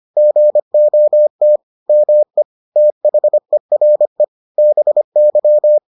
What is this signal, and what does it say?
Morse code, heard as a keyed tone, about 25 wpm, 600 Hz. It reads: GOT ME THERE BY